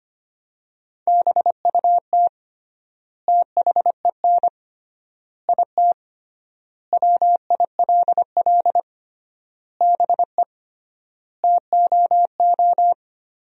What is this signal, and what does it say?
Morse code, heard as a keyed tone, about 25 wpm, 700 Hz. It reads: BUT THEN IT WILL BE TOO
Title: 3x Word-Spacing - Sentences from Top 300 Words - 25wpm